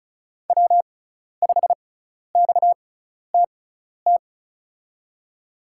Morse code, heard as a keyed tone, about 36 wpm, spaced as for 12 wpm, 700 Hz. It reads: W5XTT
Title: USA Call Signs - 6x Character Spacing - 35wpm